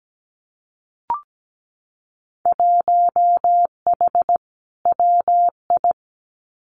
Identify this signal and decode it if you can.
Morse code, heard as a keyed tone, about 17 wpm, 700 Hz. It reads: 1H WI
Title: Winter Field Day Exchanges 17wpm